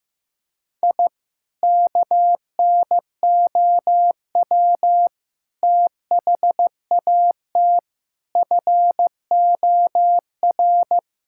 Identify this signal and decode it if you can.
Morse code, heard as a keyed tone, about 16 wpm, 700 Hz. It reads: I KNOW THAT FOR
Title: Sentences from Top 1000 Words - Review Part 5 - 15wpm